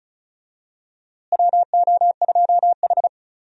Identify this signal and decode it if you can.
Morse code, heard as a keyed tone, about 36 wpm, 700 Hz. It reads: WO2H
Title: USA Call Signs - 4 Characters - Repeated 3x - 35wpm